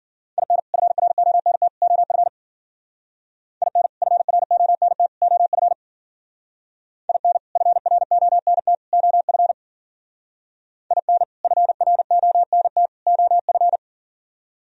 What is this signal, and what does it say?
Morse code, heard as a keyed tone, about 50 wpm, 700 Hz. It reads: IN FRONT OF IN FRONT OF IN FRONT OF IN FRONT OF
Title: Speed-Racing - Sets of 3 Words 40wpm